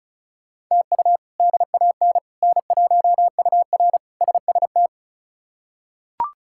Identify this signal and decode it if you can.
Morse code, heard as a keyed tone, about 34 wpm, 700 Hz. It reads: TU DAN N1UR SST
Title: K1USN SST QSO Practice - 35wpm